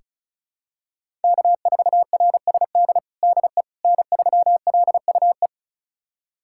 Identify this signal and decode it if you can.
Morse code, heard as a keyed tone, about 34 wpm, 700 Hz. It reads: K4RSD DE N3LUE
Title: USA Callsign DE Callsign - 5 Characters - 35wpm